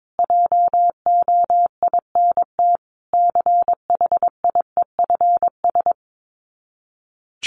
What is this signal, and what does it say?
Morse code, heard as a keyed tone, about 22 wpm, 700 Hz. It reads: JOINT CHIEFS
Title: Sets of 5 Words Encore 22wpm